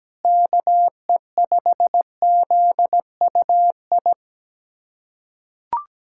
Morse code, heard as a keyed tone, about 17 wpm, 700 Hz. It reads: KE5ZUI
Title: USA Callsign DE Callsign 17wpm